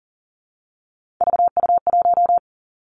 Morse code, heard as a keyed tone, about 38 wpm, 690 Hz. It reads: VU1
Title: Three Challenging Letter-Number Combinations 40wpm